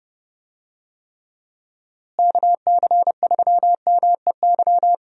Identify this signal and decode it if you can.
Morse code, heard as a keyed tone, about 30 wpm, 700 Hz. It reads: KC3MEY
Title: USA Call Signs - 6 Characters - Repeated 3x - 30wpm